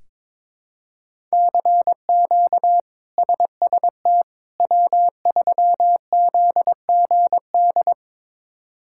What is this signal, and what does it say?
Morse code, heard as a keyed tone, about 21 wpm, 700 Hz. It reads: CQ SST W3ZGD
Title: K1USN SST QSO Practice - 22wpm